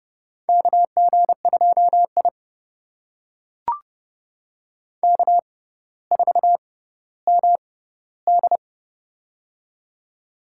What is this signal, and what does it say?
Morse code, heard as a keyed tone, about 32 wpm, spaced as for 16 wpm, 700 Hz. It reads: KG2I K4MD
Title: USA Call Signs - 4 Characters - 6x Character Spacing - 30wpm